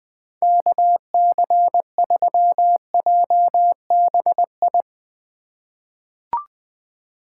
Morse code, heard as a keyed tone, about 20 wpm, 700 Hz. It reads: KC3JBI
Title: USA Call Signs - 6 Characters - 4x Character Spacing - 20wpm